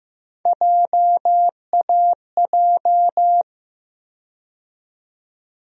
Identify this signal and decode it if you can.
Morse code, heard as a keyed tone, about 15 wpm, 700 Hz. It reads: JAJ
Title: USA Call Sign Postfixes - 3 Letters 15wpm